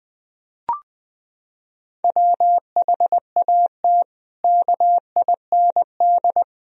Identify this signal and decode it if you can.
Morse code, heard as a keyed tone, about 20 wpm, 700 Hz. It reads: WHAT KIND
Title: Sentences from Top 1000 Words - Review Part 4 - 20wpm